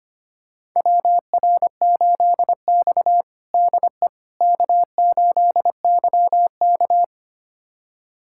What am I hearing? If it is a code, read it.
Morse code, heard as a keyed tone, about 25 wpm, 700 Hz. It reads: WR8X DE K8YK
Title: USA Callsign DE Callsign - 4 Characters - 25wpm